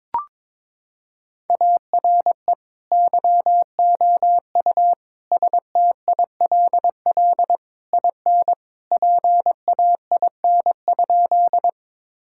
Morse code, heard as a keyed tone, about 22 wpm, 700 Hz. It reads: ARE YOU STILL IN PAIN?
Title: Sentences from Top 1600 Words 22wpm